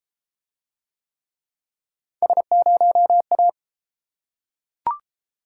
Morse code, heard as a keyed tone, about 33 wpm, 700 Hz. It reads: S0A